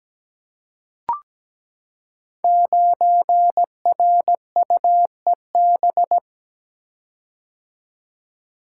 Morse code, heard as a keyed tone, about 17 wpm, 700 Hz. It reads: 9RUEB
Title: Code Groups - 5 Letter:Numbers - 17wpm